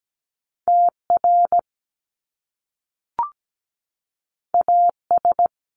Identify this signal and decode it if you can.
Morse code, heard as a keyed tone, about 17 wpm, 700 Hz. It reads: TR AS
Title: Top Two Letter Combinations in Words 17wpm